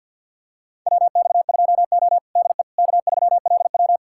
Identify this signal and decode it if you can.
Morse code, heard as a keyed tone, about 50 wpm, 700 Hz. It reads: WX2O DE K3LW
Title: USA Callsign DE Callsign - 4 Characters - 50wpm